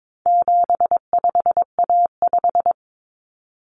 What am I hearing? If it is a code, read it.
Morse code, heard as a keyed tone, about 22 wpm, 700 Hz. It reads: 75A5
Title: Common Cut Numbers - 4-Digit 22wpm